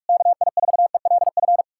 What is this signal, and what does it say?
Morse code, heard as a keyed tone, about 66 wpm, spaced as for 54 wpm, 700 Hz. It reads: KI4ELF
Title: Speed-Racing Common QSO Phrases - 45wpm